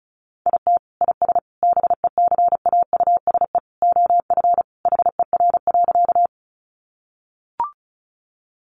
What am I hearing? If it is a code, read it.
Morse code, heard as a keyed tone, about 34 wpm, 700 Hz. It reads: IT IS BECAUSE OF HER.
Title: Sentences from Top 600 Words 35wpm